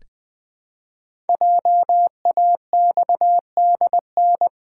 Morse code, heard as a keyed tone, about 20 wpm, 700 Hz. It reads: JAXDN
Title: Code Groups - 5 Letters - 20wpm